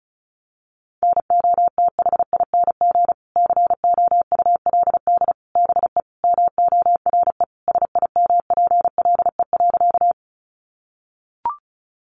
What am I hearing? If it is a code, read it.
Morse code, heard as a keyed tone, about 34 wpm, 700 Hz. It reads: NOTHING COULD BE MORE SIMPLE.
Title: Sentences from Top 500 Words 35wpm